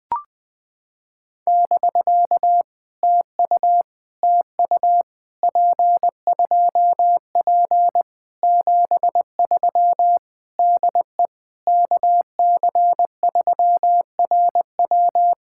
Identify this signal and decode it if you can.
Morse code, heard as a keyed tone, about 20 wpm, 700 Hz. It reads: <BK> TU TU P2P 73 DE KC3RW
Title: POTA QSO Practice - 20wpm